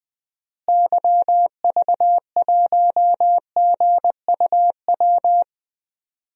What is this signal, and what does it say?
Morse code, heard as a keyed tone, about 21 wpm, 700 Hz. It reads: YV1GUW
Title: DXCC-100 Call Signs - 6 Characters - Repeated 3x - 20wpm